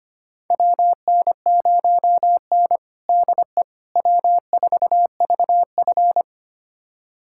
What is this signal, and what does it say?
Morse code, heard as a keyed tone, about 25 wpm, 700 Hz. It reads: WN0N DE W4VF
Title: USA Callsign DE Callsign - 4 Characters - 25wpm